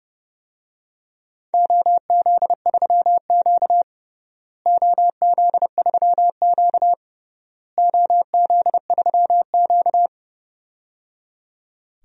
Morse code, heard as a keyed tone, about 30 wpm, 700 Hz. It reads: OZ3Q OZ3Q OZ3Q
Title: DXCC-100 Call Signs - 4 Characters - Repeated 3x - 30wpm